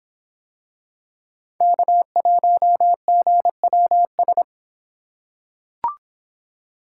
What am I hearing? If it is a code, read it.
Morse code, heard as a keyed tone, about 24 wpm, 700 Hz. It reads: K1GWS